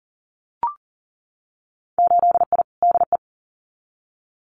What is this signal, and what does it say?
Morse code, heard as a keyed tone, about 40 wpm, 700 Hz. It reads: 8I DE
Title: Winter Field Day Exchanges Encore 40wpm